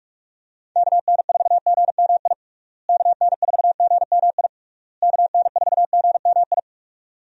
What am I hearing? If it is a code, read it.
Morse code, heard as a keyed tone, about 46 wpm, 700 Hz. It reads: KN4GMI KN4GMI KN4GMI
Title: USA Call Signs - 6 Characters - Repeated 3x - 45wpm